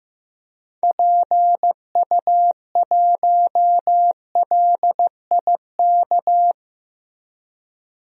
Morse code, heard as a keyed tone, about 15 wpm, 700 Hz. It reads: PU1LIK
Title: DXCC-100 Call Signs - 6 Characters - 15wpm